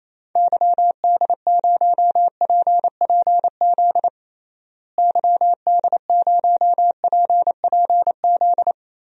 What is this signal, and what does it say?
Morse code, heard as a keyed tone, about 28 wpm, 700 Hz. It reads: YD0PPZ YD0PPZ
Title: DXCC-100 Call Signs - 6 Characters - Repeated 3x - 28wpm